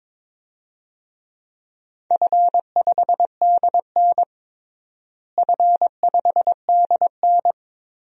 Morse code, heard as a keyed tone, about 22 wpm, 700 Hz. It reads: F5DN F5DN
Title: DXCC-100 Call Signs - 4 Characters - Repeated 2x - 22wpm